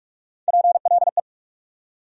Morse code, heard as a keyed tone, about 42 wpm, 700 Hz. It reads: PLE
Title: USA Call Sign Postfixes - 3 Letters 45wpm